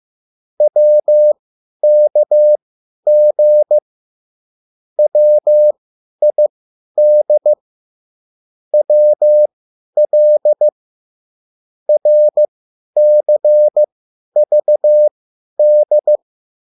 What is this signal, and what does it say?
Morse code, heard as a keyed tone, about 15 wpm, spaced as for 10 wpm, 600 Hz. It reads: WKG WID WL RCVD